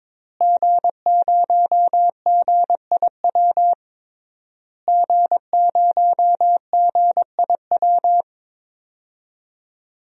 Morse code, heard as a keyed tone, about 22 wpm, 700 Hz. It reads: G0GIW G0GIW
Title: DXCC-100 Call Signs - 5 Characters - Repeated 3x - 22wpm